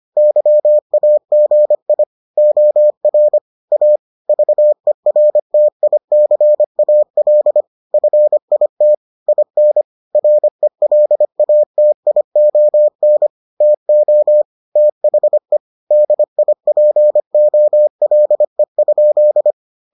Morse code, heard as a keyed tone, about 25 wpm, 600 Hz. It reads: YAGI OR A VERTICAL FIT IN RELATION TO THE DIPOLE?